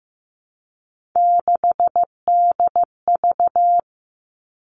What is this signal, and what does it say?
Morse code, heard as a keyed tone, about 15 wpm, 700 Hz. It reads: 6DV
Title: All Cut Numbers - 3-Digit 15wpm